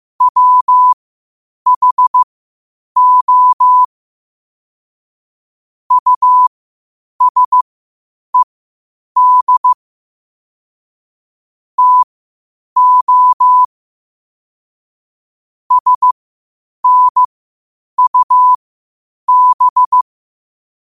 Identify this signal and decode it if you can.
Morse code, heard as a keyed tone, about 15 wpm, spaced as for 8 wpm, 1000 Hz. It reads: WHO USED TO SNUB
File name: qotdfiles / 8wpm